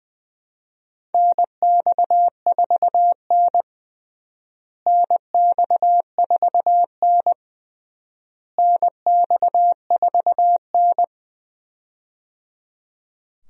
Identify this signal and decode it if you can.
Morse code, heard as a keyed tone, about 21 wpm, 700 Hz. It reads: NX4N NX4N NX4N
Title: USA Call Signs - 4 Characters - Repeated 3x - 20wpm